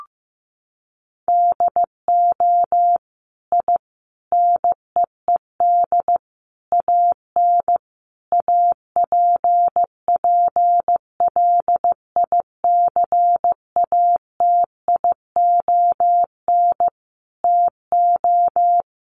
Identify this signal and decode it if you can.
Morse code, heard as a keyed tone, about 15 wpm, 700 Hz. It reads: DO I NEED AN APPLICATION TO
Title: Sentences from Top 1200 Words 15wpm